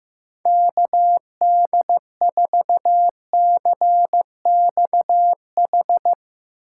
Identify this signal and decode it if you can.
Morse code, heard as a keyed tone, about 15 wpm, 700 Hz. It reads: KD4CXH
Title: USA Call Signs - 6 Characters - 2x Character Spacing - 15wpm